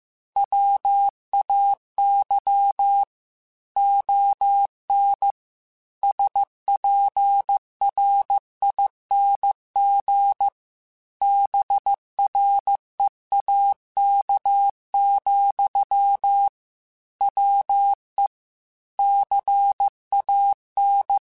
Morse code, heard as a keyed tone, about 15 wpm, 800 Hz. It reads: WAY ON SPRING BREAK, WE CAN
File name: QSOs / QSOs-mp3s / WA7QOX 1@13